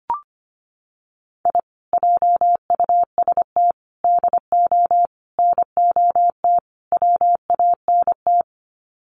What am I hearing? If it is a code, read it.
Morse code, heard as a keyed tone, about 25 wpm, 700 Hz. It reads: I JUST DO NOT WANT
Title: Sets of 5 Words Encore2 25wpm